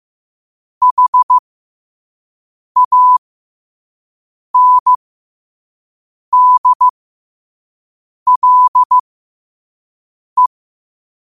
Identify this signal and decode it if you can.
Morse code, heard as a keyed tone, about 15 wpm, spaced as for 5 wpm, 1000 Hz. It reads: HANDLE
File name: qotdfiles / 5wpm